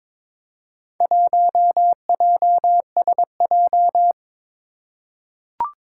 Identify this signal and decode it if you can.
Morse code, heard as a keyed tone, about 22 wpm, 700 Hz. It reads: 1JSJ
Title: Four Challenging Letter-Number Combinations 22wpm